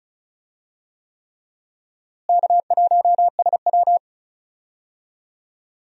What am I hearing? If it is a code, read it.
Morse code, heard as a keyed tone, about 36 wpm, 700 Hz. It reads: K1SW
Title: USA Call Signs - 4 Characters - Repeated 3x - 35wpm